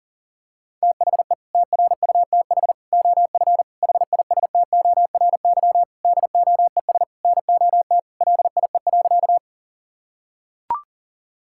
Morse code, heard as a keyed tone, about 40 wpm, 700 Hz. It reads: THE TRUTH OF HISTORY DOES NOT LIE.
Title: Sentences from Top 1600 Words 40wpm